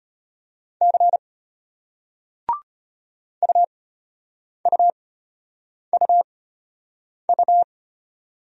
Morse code, heard as a keyed tone, about 34 wpm, spaced as for 8 wpm, 700 Hz. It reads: C UUUU